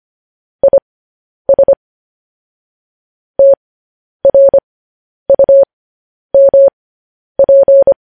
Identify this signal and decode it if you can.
Morse code, heard as a keyed tone, about 25 wpm, spaced as for 10 wpm, 550 Hz. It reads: IS TRUMP
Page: news headlines in morse code at 10 wpm